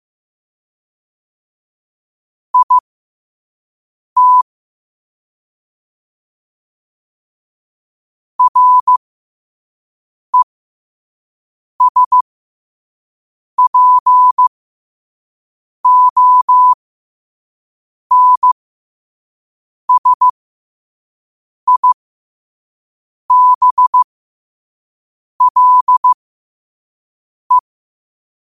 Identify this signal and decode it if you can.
Morse code, heard as a keyed tone, about 15 wpm, spaced as for 5 wpm, 1000 Hz. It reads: IT RESPONSIBLE
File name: qotdfiles / 5wpm